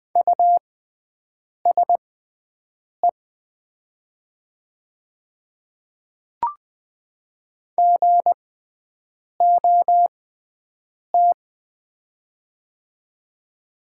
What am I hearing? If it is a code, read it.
Morse code, heard as a keyed tone, about 20 wpm, spaced as for 7 wpm, 700 Hz. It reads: USE GOT
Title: 6x Character Spacing - Top 500 Words - 20wpm